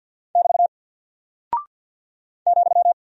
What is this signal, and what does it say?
Morse code, heard as a keyed tone, about 48 wpm, 700 Hz. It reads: <BT> ,
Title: Single Character (Letter:Number:Punctuation:Prosign) Alt 50wpm